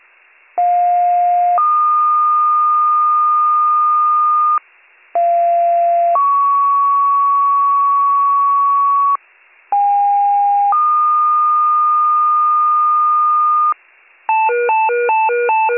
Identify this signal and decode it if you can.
Two-tone page, heard as a one-second tone, then a three-second tone, about 710 then 1220 Hz, about 690 then 1120 Hz, about 810 then 1220 Hz. It is heard from 0.6 to 13.7 s.